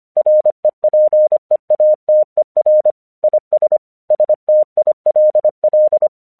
Morse code, heard as a keyed tone, about 25 wpm, 610 Hz. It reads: REPEATER IS STILL
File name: 1 English 25wpm / AQC05